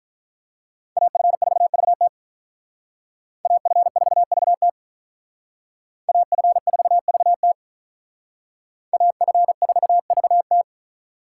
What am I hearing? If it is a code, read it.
Morse code, heard as a keyed tone, about 42 wpm, 700 Hz. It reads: AF4VT AF4VT AF4VT AF4VT
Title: Speed Racing - USA 2x2 Call Signs 35wpm